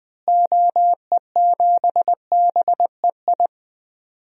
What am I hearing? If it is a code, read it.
Morse code, heard as a keyed tone, about 20 wpm, 700 Hz. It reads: OE7BEI
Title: DXCC-100 Call Signs - 6 Characters - Repeated 2x - 20wpm